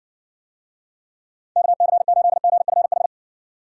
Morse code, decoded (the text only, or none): KC8GFH